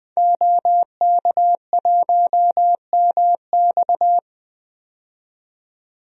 Morse code, heard as a keyed tone, about 20 wpm, 700 Hz. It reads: OK1MX
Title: DXCC-100 Call Signs - 5 Characters - 20wpm